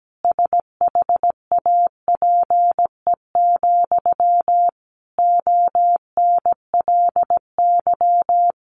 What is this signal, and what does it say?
Morse code, heard as a keyed tone, about 17 wpm, 700 Hz. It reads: SHAPE, ONLY
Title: Sentences from Top 1400 Words 17wpm